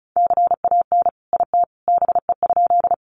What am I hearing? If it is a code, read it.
Morse code, heard as a keyed tone, about 34 wpm, 700 Hz. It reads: CAN IT BE?
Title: Sentences from Top 100 Words 35wpm